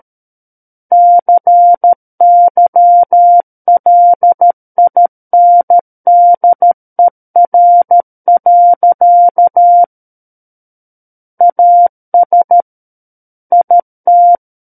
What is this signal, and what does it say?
Morse code, heard as a keyed tone, about 13 wpm, 700 Hz. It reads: CYLINDER. AS IT
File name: Book One / 04-Chapter 4 - 13 WPM